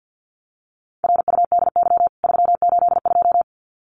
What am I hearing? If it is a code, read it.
Morse code, heard as a keyed tone, about 50 wpm, 700 Hz. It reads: RUDY 382
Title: CWT Exchange 50wpm